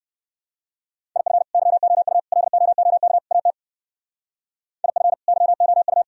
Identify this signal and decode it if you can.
Morse code, heard as a keyed tone, about 54 wpm, 700 Hz. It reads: IF YOU LOOK AT IF YOU
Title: Speed-Racing - Sets of 4 Words 45wpm